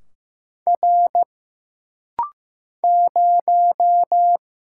Morse code, heard as a keyed tone, about 15 wpm, 700 Hz. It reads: R 0
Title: Single Letter-Number Alt - Rapid-Fire 15wpm